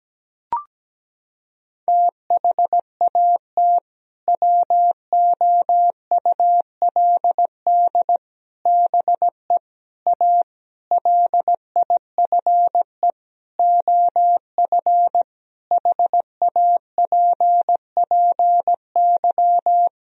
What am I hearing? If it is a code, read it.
Morse code, heard as a keyed tone, about 17 wpm, 700 Hz. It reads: THAT WOULD BE A LIFE OF HAPPY
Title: Sentences from Top 1000 Words - Review Part 4 - 17wpm